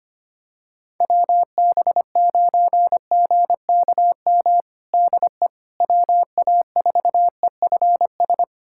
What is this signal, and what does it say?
Morse code, heard as a keyed tone, about 25 wpm, 700 Hz. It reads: WB9GKM DE WA4EFS
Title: USA Callsign DE Callsign - 6 Characters - 25wpm